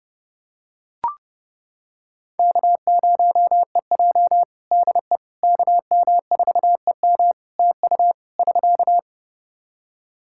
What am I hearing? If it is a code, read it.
Morse code, heard as a keyed tone, about 29 wpm, 700 Hz. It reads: K0EJ DE KM4EM TU <SK>
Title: Practice QSO - Exchange 4 - Ending - 30wpm